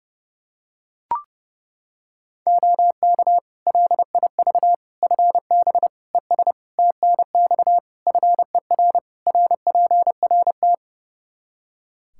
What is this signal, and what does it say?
Morse code, heard as a keyed tone, about 32 wpm, 700 Hz. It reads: OK LIV FB ES TNX FER RPRT
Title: Practice QSO - Exchange 2 - Info - 30wpm